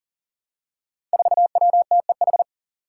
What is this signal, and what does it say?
Morse code, heard as a keyed tone, about 39 wpm, 700 Hz. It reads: 4WTEH
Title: Five Challenging Letter-Number Combinations 40wpm